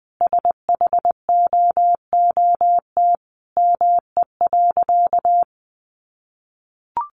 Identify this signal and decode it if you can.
Morse code, heard as a keyed tone, about 20 wpm, 700 Hz. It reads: SHOOT ME.
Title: Sentences from Top 1500 Words 20wpm